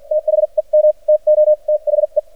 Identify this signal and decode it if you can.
Morse code, heard as a keyed tone, about 48 wpm, spaced as for 40 wpm, 610 Hz. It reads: THEM TO THE
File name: archived / CRSnet